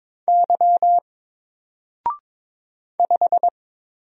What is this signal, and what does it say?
Morse code, heard as a keyed tone, about 22 wpm, 700 Hz. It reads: Y 5
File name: Single-Letter-Number-22wpm